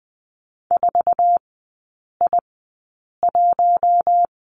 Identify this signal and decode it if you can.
Morse code, heard as a keyed tone, about 20 wpm, spaced as for 8 wpm, 700 Hz. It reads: 4I1
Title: 053 Lesson 10e - Letter r - Three Characters 20wpm